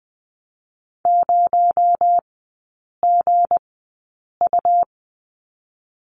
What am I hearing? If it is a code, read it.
Morse code, heard as a keyed tone, about 20 wpm, spaced as for 8 wpm, 700 Hz. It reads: 0GU
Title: 218 Lesson 35e - Number 0 - Three Characters 20wpm